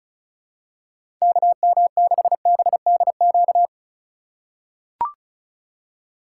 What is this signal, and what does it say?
Morse code, heard as a keyed tone, about 34 wpm, 700 Hz. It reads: KM6BDQ
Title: USA Call Signs - 6 Characters - 4x Character Spacing - 35wpm